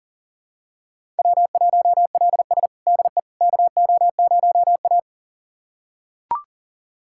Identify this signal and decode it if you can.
Morse code, heard as a keyed tone, about 40 wpm, 700 Hz. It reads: W1LS DE KO0A
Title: USA Callsign DE Callsign - 4 Characters - 40wpm